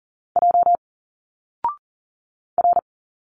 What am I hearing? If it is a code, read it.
Morse code, heard as a keyed tone, about 40 wpm, 700 Hz. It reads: J R
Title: Single Letters 40wpm